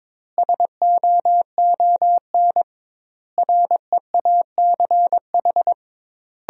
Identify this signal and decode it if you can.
Morse code, heard as a keyed tone, about 22 wpm, 700 Hz. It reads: SOON REACH
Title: 2x Word-Spacing - Sentences from Top 400 Words - 22wpm